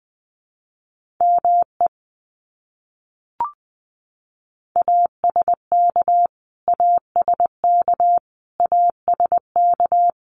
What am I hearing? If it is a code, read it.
Morse code, heard as a keyed tone, about 20 wpm, 700 Hz. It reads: ME ASK ASK ASK
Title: Top 200 Words Repeated 20wpm